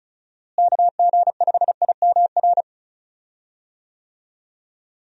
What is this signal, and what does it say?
Morse code, heard as a keyed tone, about 34 wpm, 700 Hz. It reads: KG5IMR
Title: USA Call Signs - 6 Characters - Repeated 2x - 35wpm